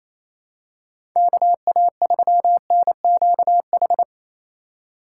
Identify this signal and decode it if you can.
Morse code, heard as a keyed tone, about 28 wpm, 700 Hz. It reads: KA3NQH
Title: USA Call Signs - 6 Characters - 4x Character Spacing - 28wpm